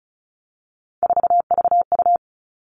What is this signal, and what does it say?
Morse code, heard as a keyed tone, about 34 wpm, 700 Hz. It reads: 4VU
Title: All Cut Numbers - 3-Digit 35wpm